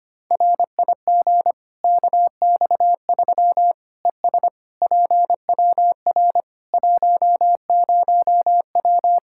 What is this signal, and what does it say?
Morse code, heard as a keyed tone, about 25 wpm, 700 Hz. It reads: RIG KX3 ES PWR 10W
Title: Practice QSO - Exchange 2 - Info - 25wpm